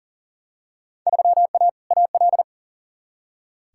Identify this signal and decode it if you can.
Morse code, heard as a keyed tone, about 37 wpm, 700 Hz. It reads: 3A AL
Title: Field Day Exchanges - 40wpm